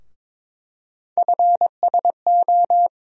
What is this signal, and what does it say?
Morse code, heard as a keyed tone, about 22 wpm, 700 Hz. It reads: FSO
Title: USA Call Sign Postfixes - 3 Letters 22wpm